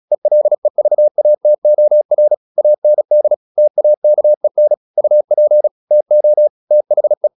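Morse code, heard as a keyed tone, about 36 wpm, 600 Hz. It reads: ELEVATOR AND TAKEN UP TO THE
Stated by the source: Morse practice at 36 wpm, text of Ken Follett's novel Never, from CW med Gnister.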